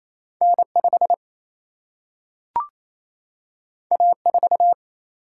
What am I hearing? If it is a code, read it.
Morse code, heard as a keyed tone, about 28 wpm, 700 Hz. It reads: N 5 A 4
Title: Single-Letter Single-Number 28wpm